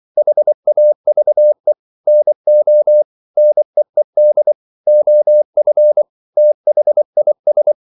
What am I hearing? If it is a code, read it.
Morse code, heard as a keyed tone, about 24 wpm, 600 Hz. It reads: HAVE NO NEED OF THIS
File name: CW / Practice / EnglishQuotations0424